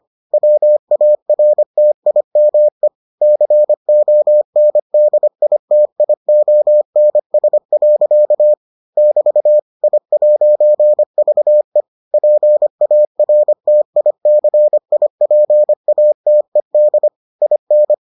Morse code, heard as a keyed tone, about 25 wpm, 600 Hz. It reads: WARTIME CONDITIONS. <BT> I'VE PARTICIPATED IN